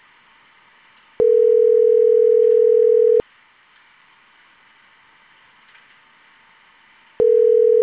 Ringback tone (North America), 440 + 480 Hz.